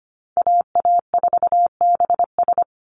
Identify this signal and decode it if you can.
Morse code, heard as a keyed tone, about 25 wpm, 700 Hz. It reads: AA4BS
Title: USA Call Signs 25wpm